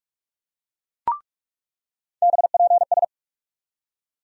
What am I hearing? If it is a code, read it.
Morse code, heard as a keyed tone, about 45 wpm, 700 Hz. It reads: BPS
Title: USA Call Sign Postfixes - 3 Letters 45wpm